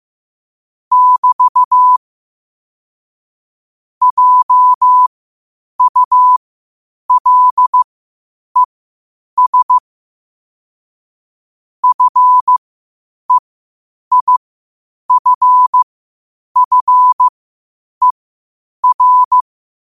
Morse code, heard as a keyed tone, about 15 wpm, spaced as for 8 wpm, 1000 Hz. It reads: <BT> JULES FEIFFER
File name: qotdfiles / 8wpm